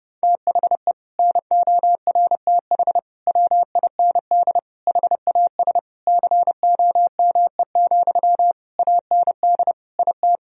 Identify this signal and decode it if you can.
Morse code, heard as a keyed tone, about 30 wpm, 700 Hz. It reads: THE NORTH WIND HAS COME, AND IT
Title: Sentences from Top 1000 Words - Review Part 3 - 30wpm